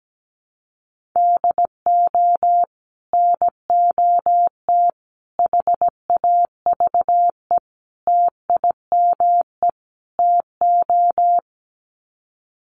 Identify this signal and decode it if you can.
Morse code, heard as a keyed tone, about 17 wpm, 700 Hz. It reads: DO NOT HAVE TIME TO
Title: Sets of 5 Words 17wpm